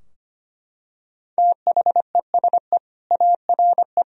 Morse code, heard as a keyed tone, about 25 wpm, 700 Hz. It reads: THESE ARE
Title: Sentences from Top 1000 Words - Review Part 3 - 25wpm